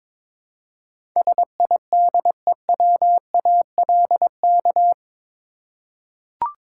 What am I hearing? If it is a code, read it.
Morse code, heard as a keyed tone, about 22 wpm, 700 Hz. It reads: SIDEWALK
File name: practice / 22wpm / Eight Letter Words - 22wpm